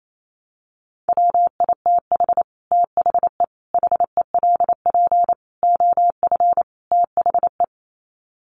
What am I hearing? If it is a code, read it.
Morse code, heard as a keyed tone, about 28 wpm, 700 Hz. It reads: WITH THE HELP OF THE
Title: Sets of 5 Words Encore 28wpm